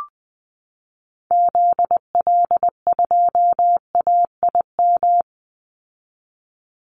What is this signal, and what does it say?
Morse code, heard as a keyed tone, about 20 wpm, 700 Hz. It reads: ZL2AIM
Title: Call Signs Repeated 2x - 20wpm